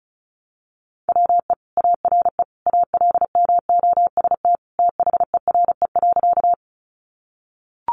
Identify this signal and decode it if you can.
Morse code, heard as a keyed tone, about 34 wpm, 700 Hz. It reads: WE ARE ALMOST THERE.
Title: Sentences from Top 700 Words 35wpm